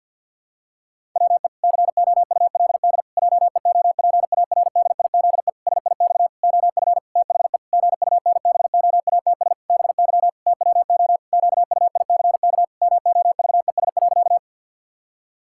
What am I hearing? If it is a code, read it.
Morse code, heard as a keyed tone, about 49 wpm, 700 Hz. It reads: WE COULD JEOPARDIZE SIX OF THE GUNBOATS BY TWO QUICK MOVES.
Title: Pangram Challenge v2 - 50wpm